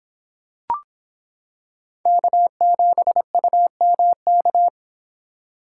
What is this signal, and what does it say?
Morse code, heard as a keyed tone, about 26 wpm, 700 Hz. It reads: K7UMK